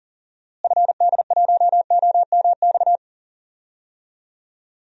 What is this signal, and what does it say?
Morse code, heard as a keyed tone, about 40 wpm, 700 Hz. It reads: FD1OMX
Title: DXCC-100 Call Signs - 6 Characters - Repeated 2x - 40wpm